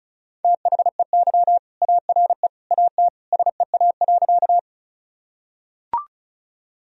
Morse code, heard as a keyed tone, about 35 wpm, 700 Hz. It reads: THEY ARE AT SEA.
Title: Sentences from Top 1000 Words - Review Part 1 - 35wpm